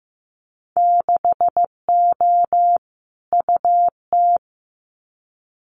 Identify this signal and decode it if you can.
Morse code, heard as a keyed tone, about 15 wpm, 700 Hz. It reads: 6O UT
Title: Winter Field Day Exchanges Encore 15wpm